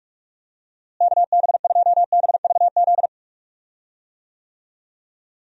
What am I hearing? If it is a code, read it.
Morse code, heard as a keyed tone, about 47 wpm, 700 Hz. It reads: KB2BVZ
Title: USA Call Signs - 6 Characters - Repeated 2x - 45wpm